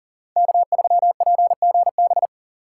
Morse code, heard as a keyed tone, about 40 wpm, 700 Hz. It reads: K3PGB
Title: USA Call Signs - 5 Characters - 4x Character Spacing - 40wpm